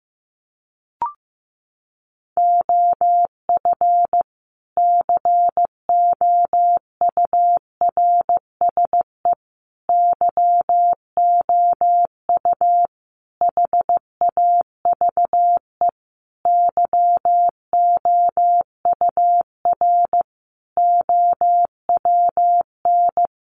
Morse code, heard as a keyed tone, about 15 wpm, 700 Hz. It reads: OF COURSE YOU HAVE YOUR OWN
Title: Sentences from Top 1200 Words 15wpm